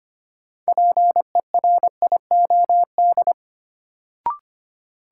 Morse code, heard as a keyed tone, about 25 wpm, 700 Hz. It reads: PERIOD
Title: Sentences from Top 800 Words - New Word Drill - 25wpm